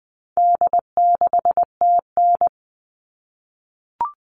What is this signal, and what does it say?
Morse code, heard as a keyed tone, about 20 wpm, 700 Hz. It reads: D6TN
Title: All Cut Numbers - 4-Digit 20wpm